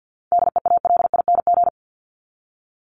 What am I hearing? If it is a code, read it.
Morse code, heard as a keyed tone, about 50 wpm, 700 Hz. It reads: DEALING
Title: ING Suffix - Top 100 Words with Base Words 50wpm